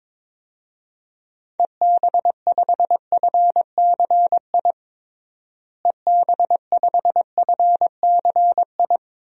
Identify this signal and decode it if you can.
Morse code, heard as a keyed tone, about 22 wpm, 700 Hz. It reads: EB5FCI EB5FCI
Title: DXCC-100 Call Signs - 6 Characters - Repeated 3x - 22wpm